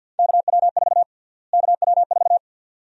Morse code, heard as a keyed tone, about 51 wpm, 700 Hz. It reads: KW4 KW4
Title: USA Call Sign Prefixes - Repeated 2x - 50wpm